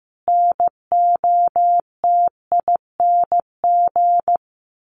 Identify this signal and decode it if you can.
Morse code, heard as a keyed tone, about 15 wpm, 700 Hz. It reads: NOTING
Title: A Sound Of Thunder 15wpm